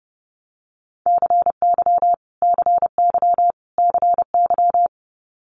Morse code, heard as a keyed tone, about 30 wpm, 700 Hz. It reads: CY CY CY
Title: Multi-letter Phonemes Repeated 3x 30wpm